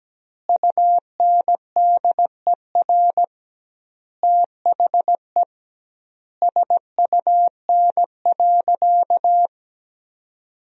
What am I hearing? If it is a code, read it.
Morse code, heard as a keyed tone, about 17 wpm, 700 Hz. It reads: UNDER THE SUN.
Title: 2x Word-Spacing - Sentences from Top 500 Words - 17wpm